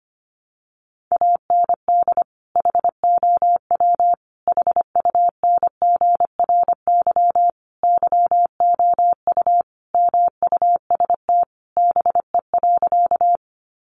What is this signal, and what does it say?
Morse code, heard as a keyed tone, about 25 wpm, 700 Hz. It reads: AND HOW HUNGRY YOU MUST BE.
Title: Sentences from Top 1400 Words 25wpm